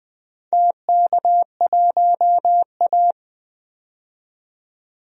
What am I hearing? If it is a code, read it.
Morse code, heard as a keyed tone, about 20 wpm, 700 Hz. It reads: TK1A
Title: DXCC-100 Call Signs - 4 Characters - 20wpm